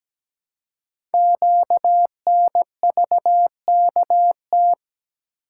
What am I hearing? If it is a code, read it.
Morse code, heard as a keyed tone, about 17 wpm, 700 Hz. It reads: QNVKT
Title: Code Groups - 5 Letter:Numbers - 17wpm